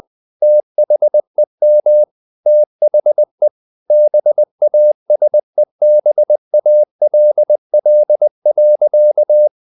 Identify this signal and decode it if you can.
Morse code, heard as a keyed tone, about 20 wpm, 600 Hz. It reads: THEM THE BASEBALL.
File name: the old man and the sea / the old man and the sea part 1 20wpm